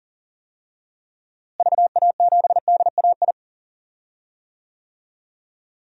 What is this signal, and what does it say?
Morse code, heard as a keyed tone, about 40 wpm, 700 Hz. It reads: VA7DAI
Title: DXCC-100 Call Signs - 6 Characters - Repeated 3x - 40wpm